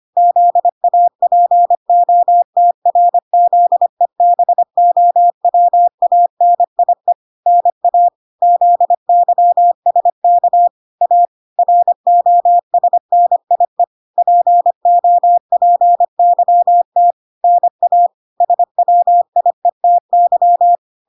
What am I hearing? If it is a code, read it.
Morse code, heard as a keyed tone, about 25 wpm, 700 Hz. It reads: ZAPOTRZEBOWANIE NA ZYSK A ROSNIE POPYT NA SWIETY